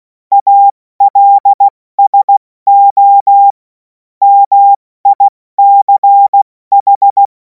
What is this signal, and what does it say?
Morse code, heard as a keyed tone, about 16 wpm, spaced as for 14 wpm, 800 Hz. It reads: ALSO MICH
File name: Morsekurs / Witze / Witz 10. Februar 2026 70 F80